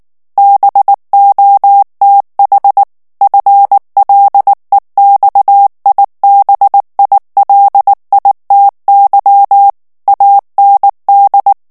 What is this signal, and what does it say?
Morse code, heard as a keyed tone, about 19 wpm, 800 Hz. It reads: BOTH FLEXIBILITY AND
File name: ANGLAIS texte clair 3 vitesse 18 mots minute